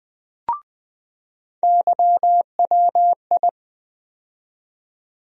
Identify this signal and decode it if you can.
Morse code, heard as a keyed tone, about 20 wpm, 700 Hz. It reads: YWI
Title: USA Call Sign Postfixes - 3 Letters 20wpm